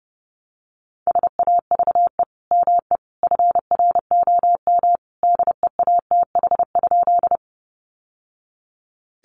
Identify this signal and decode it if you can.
Morse code, heard as a keyed tone, about 30 wpm, 700 Hz. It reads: SAVE ME FROM DEATH?
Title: Sentences from Top 1100 Words 30wpm